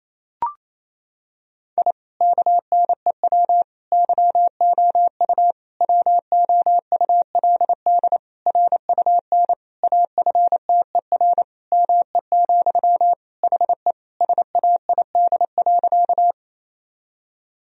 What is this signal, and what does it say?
Morse code, heard as a keyed tone, about 28 wpm, 700 Hz. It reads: I KNEW YOU WOULD RUN AFTER ME, HE SAID.
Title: Sentences from Top 1000 Words - Review Part 1 - 28wpm